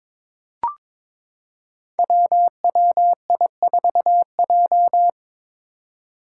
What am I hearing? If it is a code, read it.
Morse code, heard as a keyed tone, about 22 wpm, 700 Hz. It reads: WWI4J
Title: Five Challenging Letter-Number Combinations 22wpm